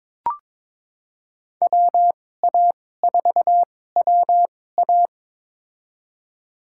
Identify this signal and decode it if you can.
Morse code, heard as a keyed tone, about 22 wpm, spaced as for 16 wpm, 700 Hz. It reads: WA4WA
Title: USA Call Signs - 5 Characters - 2x Character Spacing - 22wpm